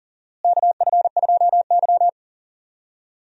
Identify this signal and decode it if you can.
Morse code, heard as a keyed tone, about 40 wpm, 700 Hz. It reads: KF2Y
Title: USA Call Signs - 4 Characters - Repeated 2x - 40wpm